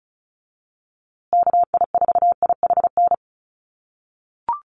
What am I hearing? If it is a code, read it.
Morse code, heard as a keyed tone, about 34 wpm, 700 Hz. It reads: KI4IHN
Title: USA Call Signs Repeated 3x 35wpm